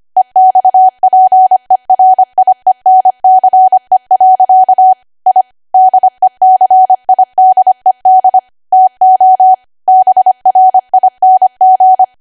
Morse code, heard as a keyed tone, about 25 wpm, 750 Hz. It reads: EXPERIENCE. I DECIDED TO BRING
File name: Archive / 25 WPM / 25WPM